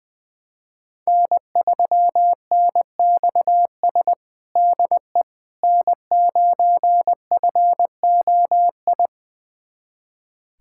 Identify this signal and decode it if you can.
Morse code, heard as a keyed tone, about 21 wpm, 700 Hz. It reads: N3NXS DE N9FOI
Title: USA Callsign DE Callsign - 5 Characters - 20wpm